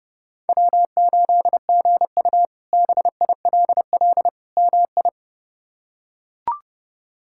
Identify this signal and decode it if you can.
Morse code, heard as a keyed tone, about 30 wpm, 700 Hz. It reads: W8GU BILL MI
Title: K1USN SST QSO Practice - 30wpm